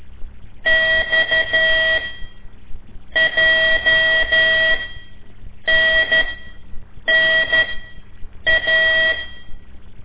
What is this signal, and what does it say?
Morse code, heard as a keyed tone, about 12 wpm, spaced as for 7 wpm, 1950 Hz. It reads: XJNNA
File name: ik1weg2 / CORSI CW / lez5-1